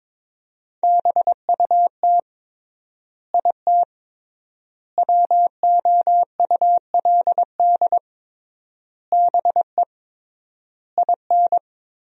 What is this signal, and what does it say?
Morse code, heard as a keyed tone, about 21 wpm, 700 Hz. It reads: BUT IT WOULD BE IN
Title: 3x Word-Spacing - Sentences from Top 300 Words - 22wpm